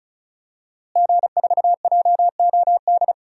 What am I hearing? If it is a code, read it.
Morse code, heard as a keyed tone, about 36 wpm, 700 Hz. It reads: G4JOD
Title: DXCC-100 Call Signs - 5 Characters - Repeated 3x - 35wpm